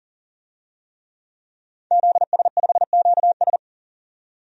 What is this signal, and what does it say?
Morse code, heard as a keyed tone, about 40 wpm, 700 Hz. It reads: ZS5QS
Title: DXCC-100 Call Signs - 5 Characters - 40wpm